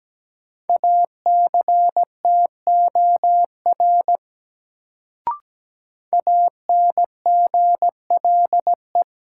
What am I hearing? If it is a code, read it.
Morse code, heard as a keyed tone, about 17 wpm, 700 Hz. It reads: ACTOR ANGLE
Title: Sentences from Top 1500 Words - New Word Drill Alt - 17wpm